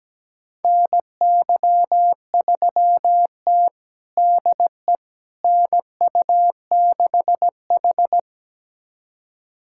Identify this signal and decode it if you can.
Morse code, heard as a keyed tone, about 17 wpm, 700 Hz. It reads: NY3T DE NU6H
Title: USA Callsign DE Callsign - 4 Characters - 17wpm